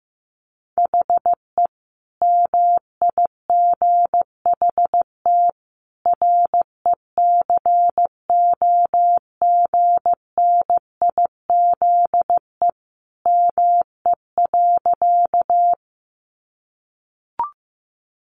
Morse code, heard as a keyed tone, about 15 wpm, 700 Hz. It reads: HE MIGHT RECOGNIZE ME.
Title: Sentences from Top 1400 Words 15wpm